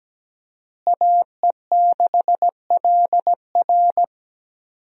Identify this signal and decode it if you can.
Morse code, heard as a keyed tone, about 17 wpm, 700 Hz. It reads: AE6LR
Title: USA Call Signs - 5 Characters - Repeated 2x - 17wpm